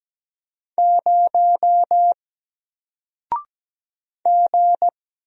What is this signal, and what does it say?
Morse code, heard as a keyed tone, about 17 wpm, 700 Hz. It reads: 0 G
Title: Single Character (Letter:Number:Punctuation:Prosign) 17wpm